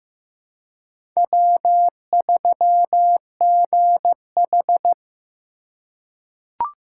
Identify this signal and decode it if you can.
Morse code, heard as a keyed tone, about 15 wpm, 700 Hz. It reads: W3GH
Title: USA Call Signs - 2x Character Spacing - 15wpm